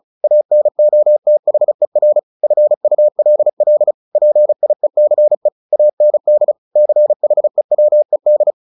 Morse code, heard as a keyed tone, about 34 wpm, 600 Hz. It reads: ANOTHER FULL PIECE AND CHEWED
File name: the old man and the sea / the old man and the sea 1 35wpm